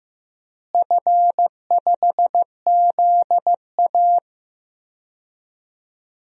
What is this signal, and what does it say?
Morse code, heard as a keyed tone, about 15 wpm, 700 Hz. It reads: F5ZA
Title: DXCC-100 Call Signs - 4 Characters - Repeated 2x - 15wpm